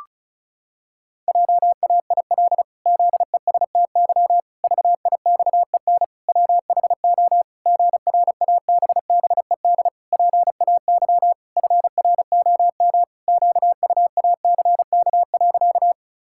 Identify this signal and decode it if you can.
Morse code, heard as a keyed tone, about 36 wpm, 700 Hz. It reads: JAIL ZESTY VIXEN WHO GRABBED PAY FROM QUACK.
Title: Pangram Challenge v2 - 35wpm